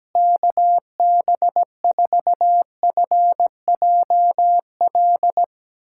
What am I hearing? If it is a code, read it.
Morse code, heard as a keyed tone, about 17 wpm, 700 Hz. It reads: KB4FJL
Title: USA Callsign DE Callsign - 6 Characters - 17wpm